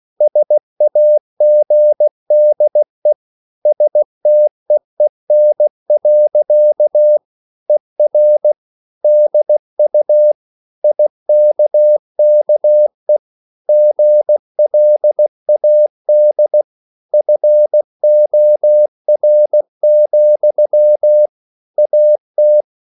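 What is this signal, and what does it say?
Morse code, heard as a keyed tone, about 16 wpm, 600 Hz. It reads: SAGDE STEEN. ER DU IKKE GLAD FOR, AT
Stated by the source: Slow (QRS) Morse code at 16 wpm, from CW med Gnister.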